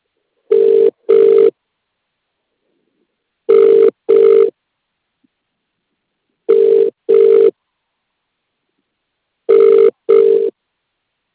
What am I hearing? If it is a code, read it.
Ringback tone (UK), 400 + 450 Hz.